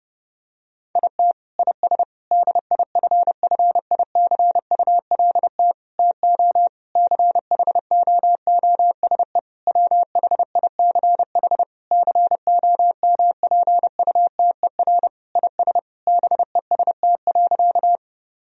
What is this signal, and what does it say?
Morse code, heard as a keyed tone, about 30 wpm, 700 Hz. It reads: IT IS DIFFICULT TO CHOOSE WHICH COMPUTER IS BEST.
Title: Sentences from Top 1000 Words - Review Part 3 - 30wpm